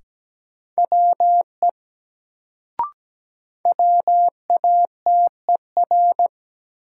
Morse code, heard as a keyed tone, about 17 wpm, 700 Hz. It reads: WE WATER
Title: Sentences from Top 100 Words - New Word Drill - 17wpm